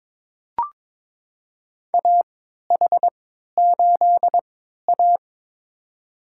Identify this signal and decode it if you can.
Morse code, heard as a keyed tone, about 21 wpm, spaced as for 12 wpm, 700 Hz. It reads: AH8A
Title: USA Call Signs - 3x Character Spacing - 22wpm